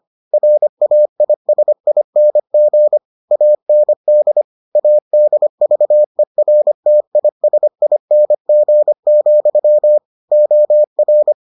Morse code, heard as a keyed tone, about 25 wpm, 600 Hz. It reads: RAISING AND ADVERTISING, OR